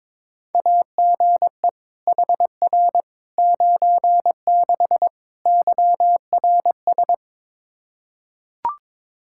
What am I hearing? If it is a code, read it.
Morse code, heard as a keyed tone, about 22 wpm, 700 Hz. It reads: AGE HR 96 YRS
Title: Practice QSO - Exchange 3 - More Info - 22wpm